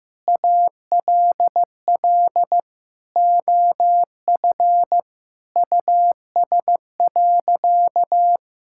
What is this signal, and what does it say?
Morse code, heard as a keyed tone, about 15 wpm, 700 Hz. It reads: ALL OF US.
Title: Sentences from Top 1000 Words - Review Part 5 - 15wpm